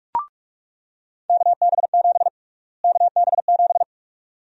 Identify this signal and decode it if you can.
Morse code, heard as a keyed tone, about 46 wpm, 700 Hz. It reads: KB7 KB7
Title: USA Call Sign Prefixes - Repeated 2x - 45wpm